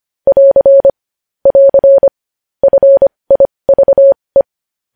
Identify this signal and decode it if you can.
Morse code, heard as a keyed tone, about 25 wpm, spaced as for 20 wpm, 550 Hz. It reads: <AR> <AR> FIVE
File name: podcast / News headlines-1x-25x20